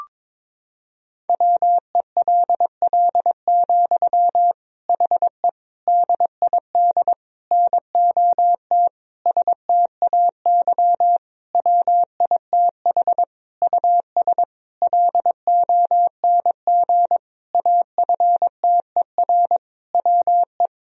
Morse code, heard as a keyed tone, about 21 wpm, 700 Hz. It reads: WELL, HE DID NOT STAY WITH US LONG AFTER WE
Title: Sentences from Top 1000 Words - Review Part 2 - 22wpm